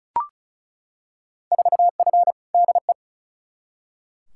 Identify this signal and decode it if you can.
Morse code, heard as a keyed tone, about 36 wpm, 700 Hz. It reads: 4F DE
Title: Field Day Exchanges Encore - 35wpm